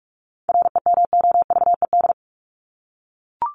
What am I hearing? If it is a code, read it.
Morse code, heard as a keyed tone, about 44 wpm, 700 Hz. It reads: REMOVED
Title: ED Suffix - Top 100 Words with Base Words 45wpm